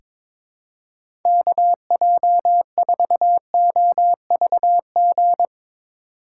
Morse code, heard as a keyed tone, about 22 wpm, 700 Hz. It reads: KJ4OVG
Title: USA Call Signs - 2x Character Spacing - 22wpm